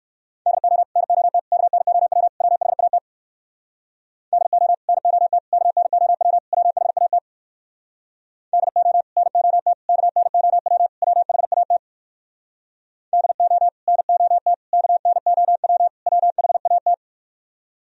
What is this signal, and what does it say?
Morse code, heard as a keyed tone, about 58 wpm, 700 Hz. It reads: DO NOT KNOW WHAT DO NOT KNOW WHAT DO NOT KNOW WHAT DO NOT KNOW WHAT